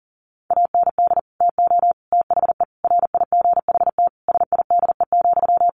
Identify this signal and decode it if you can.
Morse code, heard as a keyed tone, about 40 wpm, 700 Hz. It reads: AND TO THE RIGHT SIDE,
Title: Sentences from Top 1300 Words 40wpm